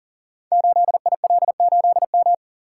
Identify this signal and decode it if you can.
Morse code, heard as a keyed tone, about 39 wpm, 700 Hz. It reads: 8IL8M
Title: Code Groups - 5 Letter:Numbers - 40wpm